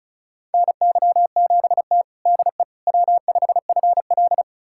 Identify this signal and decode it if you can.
Morse code, heard as a keyed tone, about 36 wpm, 700 Hz. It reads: NY7T DE W5FL
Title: USA Callsign DE Callsign - 4 Characters - 35wpm